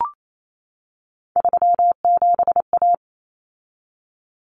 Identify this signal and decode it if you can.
Morse code, heard as a keyed tone, about 28 wpm, 700 Hz. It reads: 37A
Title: Common Cut Numbers - 3-Digit 28wpm